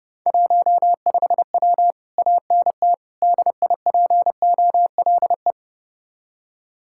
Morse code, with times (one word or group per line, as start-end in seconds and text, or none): 0.3-1.9 15W
2.2-2.9 ANT
3.2-5.5 DIPOLE